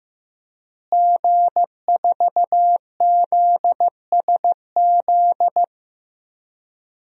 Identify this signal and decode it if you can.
Morse code, heard as a keyed tone, about 15 wpm, 700 Hz. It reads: G4ZSZ
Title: DXCC-100 Call Signs - 5 Characters - Repeated 3x - 15wpm